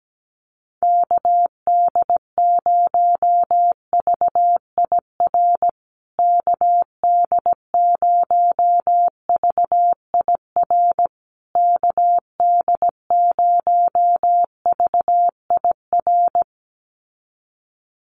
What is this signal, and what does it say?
Morse code, heard as a keyed tone, about 17 wpm, 700 Hz. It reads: KD0VIR KD0VIR KD0VIR
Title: USA Call Signs Repeated 3x 17wpm